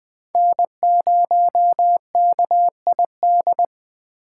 Morse code, heard as a keyed tone, about 20 wpm, 700 Hz. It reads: N0KID
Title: Speed Racing - USA Call Signs 17wpm